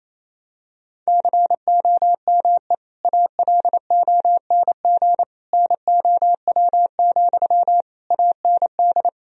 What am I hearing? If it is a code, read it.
Morse code, heard as a keyed tone, about 28 wpm, 700 Hz. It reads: COME ALONG NOW, AND
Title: Sentences from Top 1000 Words - Review Part 4 - 28wpm